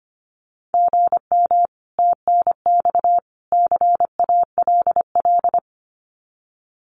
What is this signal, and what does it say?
Morse code, heard as a keyed tone, about 25 wpm, 700 Hz. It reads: GM TNX CALL
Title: Common QSO Phrases 25wpm